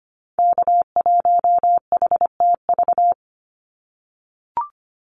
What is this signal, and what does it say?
Morse code, heard as a keyed tone, about 25 wpm, 700 Hz. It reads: K1HTV
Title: Call Signs - 25wpm